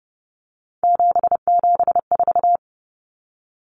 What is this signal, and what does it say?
Morse code, heard as a keyed tone, about 30 wpm, 700 Hz. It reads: 774
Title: Common Cut Numbers - 3-Digit 30wpm